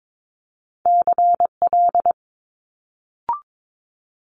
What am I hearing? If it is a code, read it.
Morse code, heard as a keyed tone, about 22 wpm, 700 Hz. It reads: CL
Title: Multi-letter Phonemes Quick Fire 22wpm